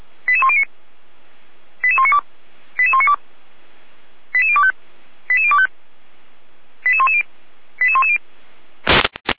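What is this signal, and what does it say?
Five-tone selective call (ZVEI), heard as quick runs of five tones, tones of about 70 ms. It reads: 88109 88192 88192 88025 88025 88110 88110
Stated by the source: ZVEI five-tone selective call 88104 received by SDR, fire brigade Feuerwehr Orlamünde.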